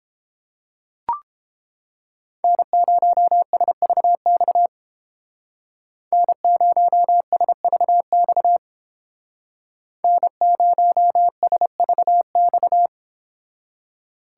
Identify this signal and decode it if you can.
Morse code, heard as a keyed tone, about 29 wpm, 700 Hz. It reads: N0SVX N0SVX N0SVX